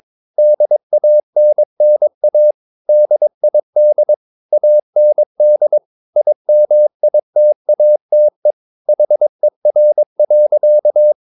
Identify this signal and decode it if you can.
Morse code, heard as a keyed tone, about 22 wpm, 600 Hz. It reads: DANNA DID AND IMITATE HER.
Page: Romanen Never af Ken Follet på engelsk.